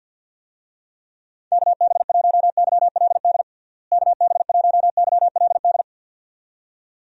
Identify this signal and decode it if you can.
Morse code, heard as a keyed tone, about 48 wpm, 700 Hz. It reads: KB1YLD KB1YLD
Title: USA Call Signs - 6 Characters - Repeated 2x - 50wpm